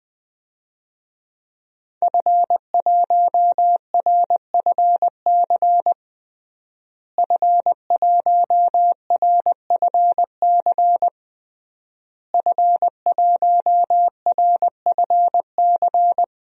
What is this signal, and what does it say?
Morse code, heard as a keyed tone, about 20 wpm, 700 Hz. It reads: F1RFC F1RFC F1RFC
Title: DXCC-100 Call Signs - 5 Characters - Repeated 3x - 20wpm